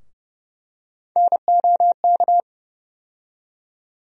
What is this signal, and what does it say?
Morse code, heard as a keyed tone, about 32 wpm, 700 Hz. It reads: NOK
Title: USA Call Sign Postfixes - 3 Letters 30wpm